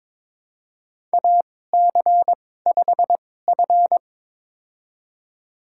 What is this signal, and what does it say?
Morse code, heard as a keyed tone, about 22 wpm, spaced as for 16 wpm, 700 Hz. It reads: AC5F
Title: USA Call Signs - 4 Characters - 2x Character Spacing - 22wpm